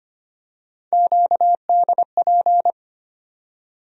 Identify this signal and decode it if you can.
Morse code, heard as a keyed tone, about 25 wpm, 700 Hz. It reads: QDP
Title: USA Call Sign Postfixes - 3 Letters 25wpm